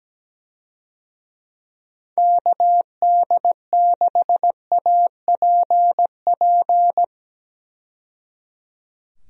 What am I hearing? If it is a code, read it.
Morse code, heard as a keyed tone, about 17 wpm, 700 Hz. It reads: KD6APP
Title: USA Call Signs - 6 Characters - 17wpm